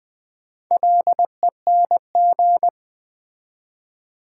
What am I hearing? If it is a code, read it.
Morse code, heard as a keyed tone, about 20 wpm, 690 Hz. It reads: LENG
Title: Build A Word - 2 Letters At A Time - 6-Letter Words - 20wpm